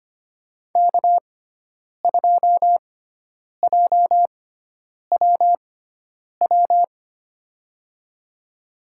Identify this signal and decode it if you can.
Morse code, heard as a keyed tone, about 25 wpm, spaced as for 9 wpm, 690 Hz. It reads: K2JWW
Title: USA Call Signs - 5 Characters - 6x Character Spacing - 25wpm